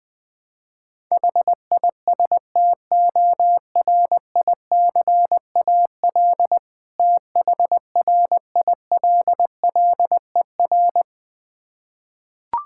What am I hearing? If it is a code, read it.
Morse code, heard as a keyed tone, about 20 wpm, 690 Hz. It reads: HISTORICAL THRILLER
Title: Difficult Words with Hints - 20wpm